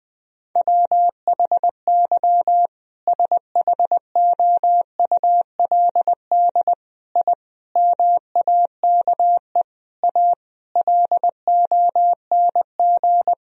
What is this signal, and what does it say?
Morse code, heard as a keyed tone, about 20 wpm, 700 Hz. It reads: WHY SHOULD I MAKE A LONG
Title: Sentences from Top 1000 Words - Review Part 1 - 20wpm